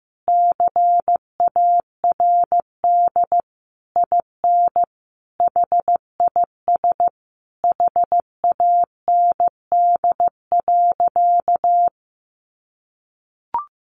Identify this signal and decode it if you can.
Morse code, heard as a keyed tone, about 15 wpm, 700 Hz. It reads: CARD IN HIS HAND.
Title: Sentences from Top 600 Words 15wpm